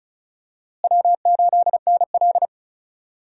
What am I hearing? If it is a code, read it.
Morse code, heard as a keyed tone, about 34 wpm, 700 Hz. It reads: W8NL
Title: USA Call Signs - 4 Characters - Repeated 3x - 35wpm